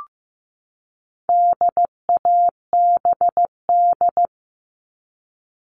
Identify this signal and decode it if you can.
Morse code, heard as a keyed tone, about 15 wpm, 700 Hz. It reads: DABD
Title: All Cut Numbers - 4-Digit 15wpm